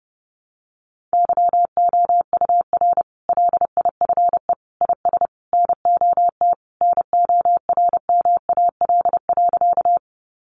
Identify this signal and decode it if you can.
Morse code, heard as a keyed tone, about 30 wpm, 700 Hz. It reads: YOUR LIFE IS NOT NORMAL.
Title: Sentences from Top 1200 Words 30wpm